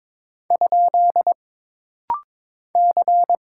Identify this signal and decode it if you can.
Morse code, heard as a keyed tone, about 22 wpm, 700 Hz. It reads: ? C
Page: This rapid-fire timing is geared to push your ICR (Instant Character Recognition) skills.